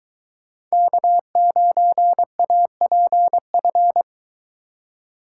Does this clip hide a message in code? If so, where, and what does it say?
Yes: Morse code from 0.7 to 4.0 s, heard as a keyed tone, about 23 wpm, 700 Hz. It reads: K9APF